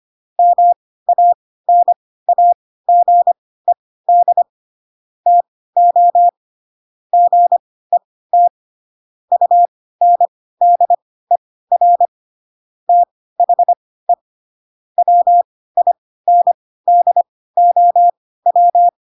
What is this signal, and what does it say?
Morse code, heard as a keyed tone, about 25 wpm, spaced as for 16 wpm, 700 Hz. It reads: MANAGED TO GET UNDER THE WINDOW
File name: Pinocchio / Pinocchio ch6 10 25 ik1hge